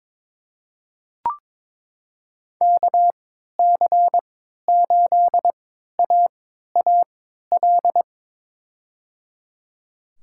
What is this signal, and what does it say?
Morse code, heard as a keyed tone, about 22 wpm, spaced as for 12 wpm, 700 Hz. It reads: KC8AAL
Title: USA Call Signs - 6 Characters - 3x Character Spacing - 22wpm